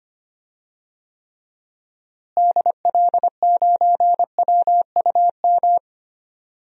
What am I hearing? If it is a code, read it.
Morse code, heard as a keyed tone, about 25 wpm, 700 Hz. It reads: DL9WUM
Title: DXCC-100 Call Signs - 6 Characters - Repeated 2x - 25wpm